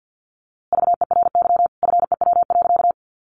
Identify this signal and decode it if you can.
Morse code, heard as a keyed tone, about 50 wpm, 700 Hz. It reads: VERY FEW.
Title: A Sound Of Thunder 50wpm